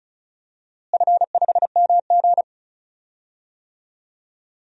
Morse code, heard as a keyed tone, about 36 wpm, 700 Hz. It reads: F5MG
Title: DXCC-100 Call Signs - 4 Characters - Repeated 2x - 35wpm